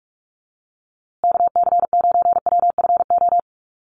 Morse code, heard as a keyed tone, about 44 wpm, 700 Hz. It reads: KC9WFO
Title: USA Call Signs Repeated 3x 45wpm